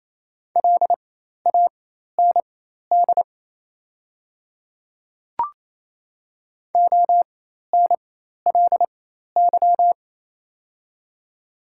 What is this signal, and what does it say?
Morse code, heard as a keyed tone, about 28 wpm, spaced as for 13 wpm, 700 Hz. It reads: LAND ONLY
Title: 4x Character Spacing - Top 300 Words - 28wpm